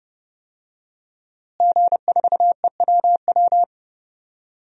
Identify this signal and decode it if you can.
Morse code, heard as a keyed tone, about 31 wpm, 700 Hz. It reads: G4EWW
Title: DXCC-100 Call Signs - 5 Characters - Repeated 2x - 30wpm